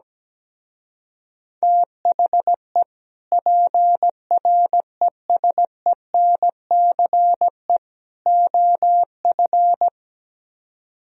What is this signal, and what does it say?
Morse code, heard as a keyed tone, about 17 wpm, 700 Hz. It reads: THE PRESENCE OF
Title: Speed-Racing - Sets of 3 Words 17wpm